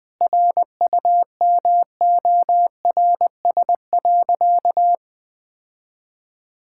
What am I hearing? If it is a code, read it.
Morse code, heard as a keyed tone, about 20 wpm, 700 Hz. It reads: RUMORS.
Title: Sentences with 6-Word Phrases - 20wpm